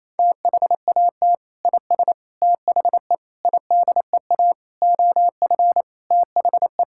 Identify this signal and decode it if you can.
Morse code, heard as a keyed tone, about 28 wpm, 700 Hz. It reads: THAT IS THE IDEA OF THE
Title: Sentences from Top 1000 Words - Review Part 5 - 28wpm